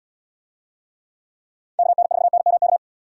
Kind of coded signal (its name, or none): Morse code